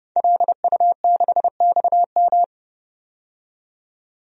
Morse code, heard as a keyed tone, about 29 wpm, 690 Hz. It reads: LU6XM
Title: DXCC-100 Call Signs - 5 Characters - 30wpm